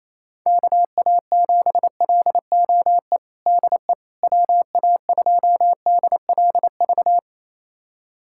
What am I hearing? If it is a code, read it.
Morse code, heard as a keyed tone, about 28 wpm, 700 Hz. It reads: KA7LOE DE WA2DLV
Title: USA Callsign DE Callsign - 6 Characters - 28wpm